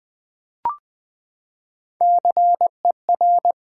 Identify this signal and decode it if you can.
Morse code, heard as a keyed tone, about 20 wpm, 700 Hz. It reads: CER
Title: Build A Word - 3 Letters At A Time - 9-Letter Words - 20wpm